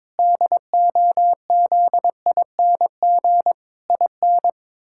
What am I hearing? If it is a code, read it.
Morse code, heard as a keyed tone, about 22 wpm, 700 Hz. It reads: DOZING IN
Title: Pangram Challenge v2 - 22wpm